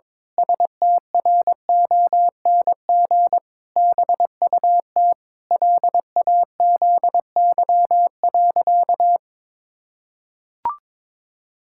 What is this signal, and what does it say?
Morse code, heard as a keyed tone, about 22 wpm, 700 Hz. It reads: STRONG BUT LAZY.
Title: Pangram Challenge v2 - 22wpm